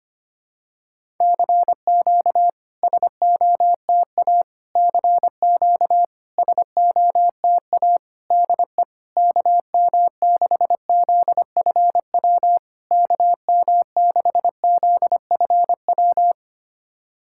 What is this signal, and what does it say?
Morse code, heard as a keyed tone, about 26 wpm, 700 Hz. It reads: CQ SOTA CQ SOTA DE KM6ZFW KM6ZFW
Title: SOTA QSO Practice 25wpm